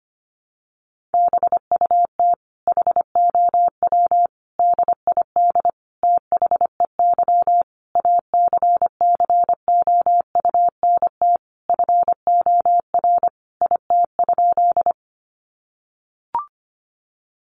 Morse code, heard as a keyed tone, about 25 wpm, 700 Hz. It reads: BUT HOW DID THEY ACCOUNT FOR IT?
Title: Sentences from Top 900 Words 25wpm